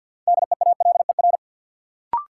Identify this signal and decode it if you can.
Morse code, heard as a keyed tone, about 52 wpm, 700 Hz. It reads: DEALER
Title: Sentences from Top 1100 Words - New Word Drill - 50wpm